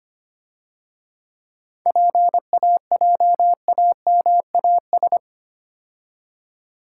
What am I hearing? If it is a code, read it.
Morse code, heard as a keyed tone, about 30 wpm, 700 Hz. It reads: PAJAMAS
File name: Easy Words - Speed-Racing with Hints - 25wpm